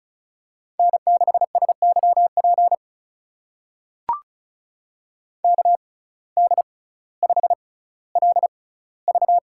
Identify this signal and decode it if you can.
Morse code, heard as a keyed tone, about 36 wpm, spaced as for 18 wpm, 700 Hz. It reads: N6SYP KD5LV
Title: USA Call Signs - 5 Characters - 6x Character Spacing - 35wpm